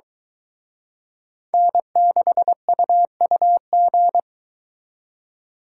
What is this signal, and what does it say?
Morse code, heard as a keyed tone, about 23 wpm, 700 Hz. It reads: N6UUG